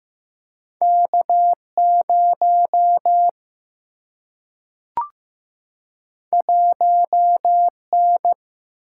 Morse code, heard as a keyed tone, about 15 wpm, 700 Hz. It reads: K 0 1 N
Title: Single-Letter Single-Number 15wpm